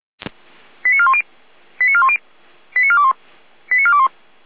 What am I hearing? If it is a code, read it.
Five-tone selective call (ZVEI), heard as quick runs of five tones, tones of about 70 ms. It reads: 89410 89410 89421 89421
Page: SDR/ZVEI